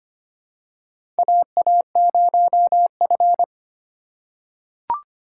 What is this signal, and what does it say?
Morse code, heard as a keyed tone, about 25 wpm, 700 Hz. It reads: AA0F
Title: USA Call Signs - 4 Characters - 3x Character Spacing - 25wpm